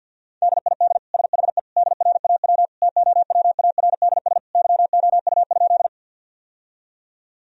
Morse code, heard as a keyed tone, about 47 wpm, 700 Hz. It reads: DID SHE DRAW TOWARDS YOU?
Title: Sentences from Top 1000 Words - Review Part 2 - 50wpm